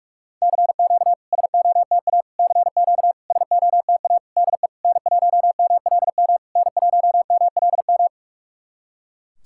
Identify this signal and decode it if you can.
Morse code, heard as a keyed tone, about 44 wpm, 690 Hz. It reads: CQ SOTA CQ SOTA DE N1MLM N1MLM
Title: SOTA QSO Practice 45wpm